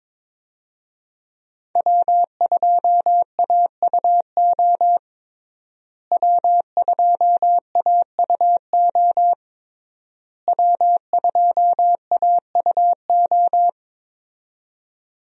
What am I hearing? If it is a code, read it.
Morse code, heard as a keyed tone, about 22 wpm, 700 Hz. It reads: W2AUO W2AUO W2AUO
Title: USA Call Signs - 5 Characters - Repeated 3x - 22wpm